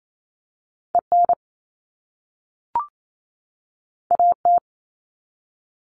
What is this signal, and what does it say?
Morse code, heard as a keyed tone, about 28 wpm, 700 Hz. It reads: EN AT
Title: All Cut Numbers - 2-Digit 28wpm